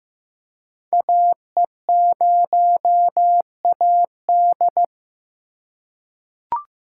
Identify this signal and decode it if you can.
Morse code, heard as a keyed tone, about 15 wpm, 700 Hz. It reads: AE0AD
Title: Code Groups - 5 Letter:Numbers - 15wpm